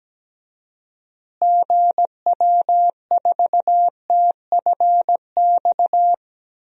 Morse code, heard as a keyed tone, about 17 wpm, 700 Hz. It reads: GW4TFX
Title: DXCC-100 Call Signs - 6 Characters - Repeated 2x - 17wpm